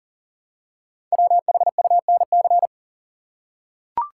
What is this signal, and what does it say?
Morse code, heard as a keyed tone, about 38 wpm, 700 Hz. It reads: WHUNC
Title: Code Groups - 5 Letters - 40wpm